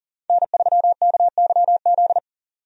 Morse code, heard as a keyed tone, about 40 wpm, 700 Hz. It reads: N3KYZ
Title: USA Call Signs - 5 Characters - 40wpm